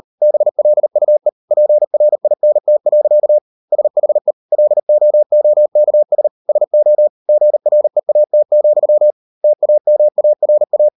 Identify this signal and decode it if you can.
Morse code, heard as a keyed tone, about 40 wpm, 600 Hz. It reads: BLUE PRINT. SHE LOOKS SO GREAT, TAMARA